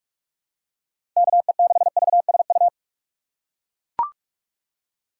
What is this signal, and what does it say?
Morse code, heard as a keyed tone, about 44 wpm, 700 Hz. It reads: KE6VSU
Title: USA Call Signs - 6 Characters - 2x Character Spacing - 45wpm